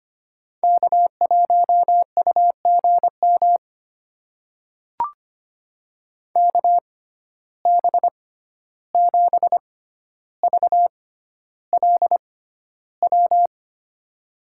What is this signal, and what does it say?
Morse code, heard as a keyed tone, about 25 wpm, spaced as for 12 wpm, 700 Hz. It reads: K1UGM KB7VLW
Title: USA Call Signs - 6x Character Spacing - 25wpm